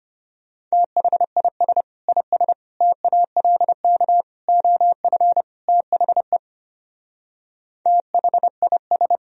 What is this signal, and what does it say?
Morse code, heard as a keyed tone, about 27 wpm, 700 Hz. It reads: THIS IS TALK OF THE THIS
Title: Speed-Racing - Sets of 5 Words 25wpm